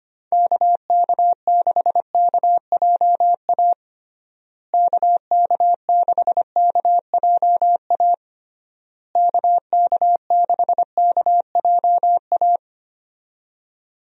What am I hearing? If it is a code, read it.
Morse code, heard as a keyed tone, about 25 wpm, 700 Hz. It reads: KK6KJA KK6KJA KK6KJA
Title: USA Call Signs - 6 Characters - Repeated 3x - 25wpm